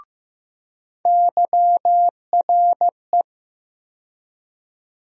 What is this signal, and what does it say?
Morse code, heard as a keyed tone, about 15 wpm, 700 Hz. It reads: YRE
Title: USA Call Sign Postfixes - 3 Letters 15wpm